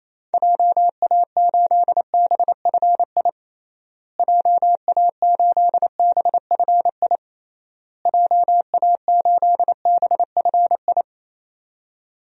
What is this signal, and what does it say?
Morse code, heard as a keyed tone, about 28 wpm, 700 Hz. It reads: JA8BFI JA8BFI JA8BFI
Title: DXCC-100 Call Signs - 6 Characters - Repeated 3x - 28wpm